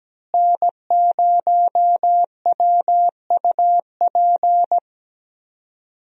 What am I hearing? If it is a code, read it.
Morse code, heard as a keyed tone, about 17 wpm, 700 Hz. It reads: N0WUP
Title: USA Call Signs - 5 Characters - Repeated 2x - 17wpm